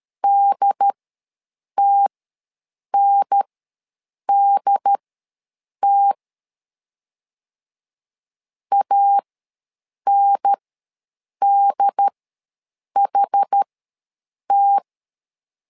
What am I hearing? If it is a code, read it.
Morse code, heard as a keyed tone, about 13 wpm, spaced as for 7 wpm, 800 Hz. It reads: DTNDT ANDHT
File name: LEZIONI / 2 UHDN / 2 UHDN ws 12wpm 1708